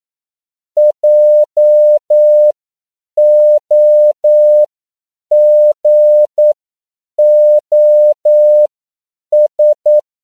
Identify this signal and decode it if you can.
Morse code, heard as a keyed tone, about 9 wpm, spaced as for 7 wpm, 600 Hz. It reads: JOGOS